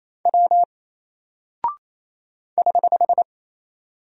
Morse code, heard as a keyed tone, about 28 wpm, 700 Hz. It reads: W <HH>
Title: Single Character (Letter:Number:Punctuation:Prosign) Alt 28wpm